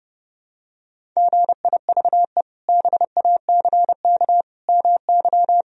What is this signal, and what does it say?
Morse code, heard as a keyed tone, about 30 wpm, 700 Hz. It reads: GIVE BACK MY
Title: Sentences from Top 1000 Words - Review Part 1 - 30wpm